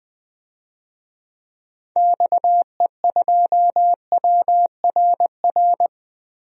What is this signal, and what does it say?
Morse code, heard as a keyed tone, about 20 wpm, 700 Hz. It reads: XE2WRR
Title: DXCC-100 Call Signs - 6 Characters - Repeated 2x - 20wpm